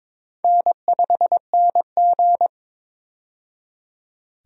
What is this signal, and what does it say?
Morse code, heard as a keyed tone, about 22 wpm, 700 Hz. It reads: N5NG
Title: USA Call Signs - 4 Characters - 22wpm